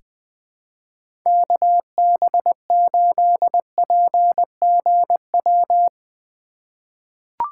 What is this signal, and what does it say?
Morse code, heard as a keyed tone, about 20 wpm, 700 Hz. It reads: KB8PGW
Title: K1USN SST QSO Practice - 20wpm